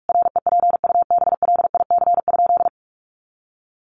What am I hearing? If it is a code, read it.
Morse code, heard as a keyed tone, about 44 wpm, 700 Hz. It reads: REPUBLIC?
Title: Sentences from Top 1500 Words 45wpm